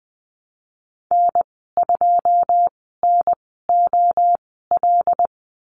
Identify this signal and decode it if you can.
Morse code, heard as a keyed tone, about 20 wpm, spaced as for 14 wpm, 700 Hz. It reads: N2NOL
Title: 082 Lesson 14g - Number 2 - Call Signs 20wpm